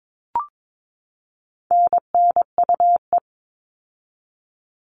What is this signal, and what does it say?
Morse code, heard as a keyed tone, about 22 wpm, 700 Hz. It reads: NNUE
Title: All Cut Numbers - 4-Digit 22wpm